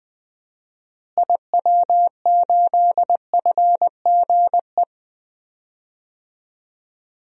Morse code, heard as a keyed tone, about 20 wpm, 700 Hz. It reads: IW8FGE
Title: DXCC-100 Call Signs - 6 Characters - Repeated 2x - 20wpm